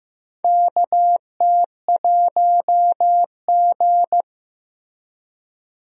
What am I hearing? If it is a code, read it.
Morse code, heard as a keyed tone, about 15 wpm, 700 Hz. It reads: KT1G
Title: USA Call Signs - 4 Characters - Repeated 2x - 15wpm